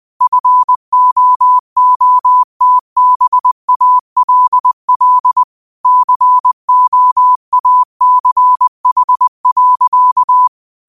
Morse code, heard as a keyed tone, about 20 wpm, 1000 Hz. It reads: FOOTBALL COACH.
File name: qotdfiles / 20wpm